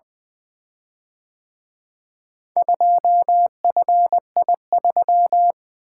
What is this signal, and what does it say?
Morse code, heard as a keyed tone, about 24 wpm, 700 Hz. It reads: 2FI3
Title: Speed-Racing - Four Letter-Number Combinations - 20wpm